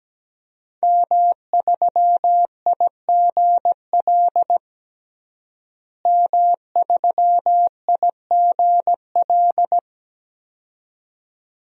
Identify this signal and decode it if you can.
Morse code, heard as a keyed tone, about 17 wpm, 700 Hz. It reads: M3IGL M3IGL
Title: DXCC-100 Call Signs - 5 Characters - Repeated 2x - 17wpm